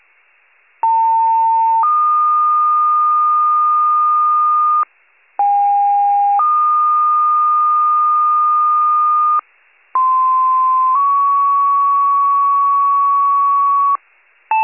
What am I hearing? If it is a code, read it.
Two-tone page, heard as a one-second tone, then a three-second tone, about 900 then 1220 Hz, about 810 then 1220 Hz, about 1040 then 1120 Hz. It is heard from 0.8 to 14.0 s.